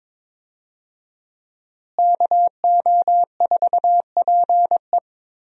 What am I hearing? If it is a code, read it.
Morse code, heard as a keyed tone, about 22 wpm, 700 Hz. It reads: KO4PE
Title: USA Call Signs - 5 Characters - Repeated 2x - 22wpm